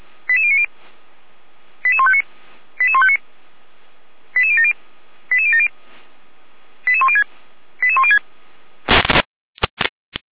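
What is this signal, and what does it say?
Five-tone selective call (ZVEI), heard as quick runs of five tones, tones of about 70 ms. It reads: 88090 88159 88159 88070 88070 88106 88106
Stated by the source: ZVEI five-tone selective call 88012, received by SDR, logged by Feuerwehr Orlamünde fire brigade.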